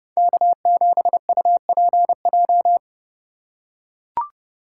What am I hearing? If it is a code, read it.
Morse code, heard as a keyed tone, about 30 wpm, 700 Hz. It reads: K7UPJ
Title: K1USN SST QSO Practice - 30wpm